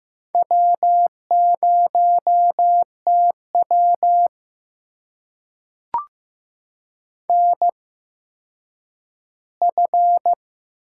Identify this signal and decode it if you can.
Morse code, heard as a keyed tone, about 15 wpm, spaced as for 9 wpm, 700 Hz. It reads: W0TW NF
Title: USA Call Signs - 4 Characters - 8x Character Spacing - 15wpm